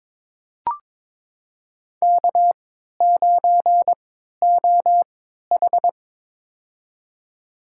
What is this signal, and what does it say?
Morse code, heard as a keyed tone, about 21 wpm, spaced as for 12 wpm, 700 Hz. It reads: K9OH
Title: USA Call Signs - 4 Characters - 3x Character Spacing - 22wpm